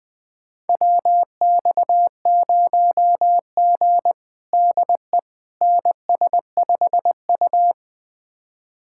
Morse code, heard as a keyed tone, about 20 wpm, 700 Hz. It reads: WX0G DE NS5U
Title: USA Callsign DE Callsign - 4 Characters - 20wpm